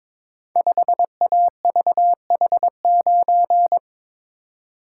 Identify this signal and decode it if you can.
Morse code, heard as a keyed tone, about 21 wpm, 700 Hz. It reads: 5AVH9
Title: Code Groups - 5 Letter:Numbers - 22wpm